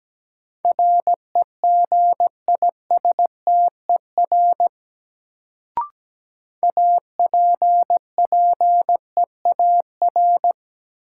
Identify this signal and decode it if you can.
Morse code, heard as a keyed tone, about 17 wpm, 700 Hz. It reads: REGISTER APPEAR
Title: Sentences from Top 1200 Words - New Word Drill Alt - 17wpm